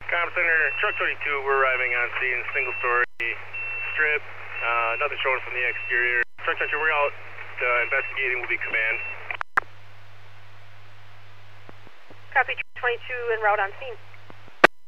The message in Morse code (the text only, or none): none